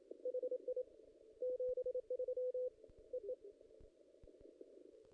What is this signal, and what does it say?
Morse code, heard as a keyed tone, about 28 wpm, 500 Hz. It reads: HI 73 EE
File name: contests / ss-cw-k5tr-op-n5rz -2200